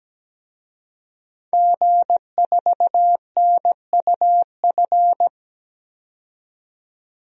Morse code, heard as a keyed tone, about 17 wpm, 700 Hz. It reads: G4NUF
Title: DXCC-100 Call Signs - 5 Characters - 17wpm